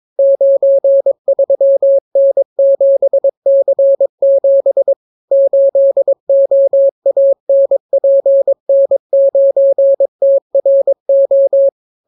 Morse code, heard as a keyed tone, about 22 wpm, 550 Hz. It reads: 93N7C7 8OANPN9TRO